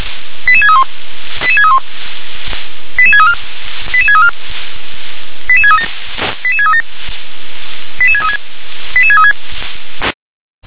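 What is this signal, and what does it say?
Five-tone selective call (ZVEI), heard as quick runs of five tones, tones of about 70 ms. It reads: 88631 88631 88634 88634 88637 88637 88636 88636
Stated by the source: ZVEI five-tone selective call received on an SDR.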